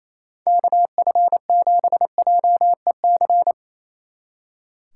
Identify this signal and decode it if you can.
Morse code, heard as a keyed tone, about 34 wpm, 700 Hz. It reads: KF7JEC
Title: Speed Racing - USA 2x3 Call Signs 28wpm